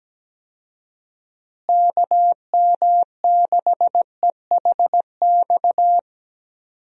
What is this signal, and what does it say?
Morse code, heard as a keyed tone, about 17 wpm, 700 Hz. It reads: KM6EHX
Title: USA Call Signs - 6 Characters - Repeated 2x - 17wpm